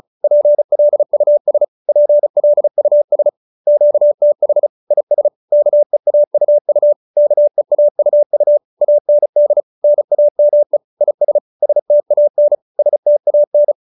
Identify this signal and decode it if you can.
Morse code, heard as a keyed tone, about 34 wpm, 590 Hz. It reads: PLUS PLUS QTH IS KEAUU KEAUU AND NAME IS STAN STAN